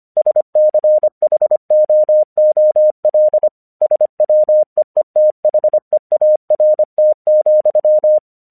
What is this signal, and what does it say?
Morse code, heard as a keyed tone, about 25 wpm, 610 Hz. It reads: SCHOOL SWEETHEART,
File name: 1 English 25wpm / AQC10